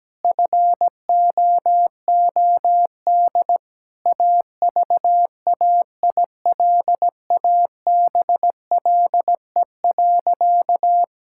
Morse code, heard as a keyed tone, about 17 wpm, 700 Hz. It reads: FOOD AVAILABLE.
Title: Sentences from Top 1000 Words - Review Part 6 - 17wpm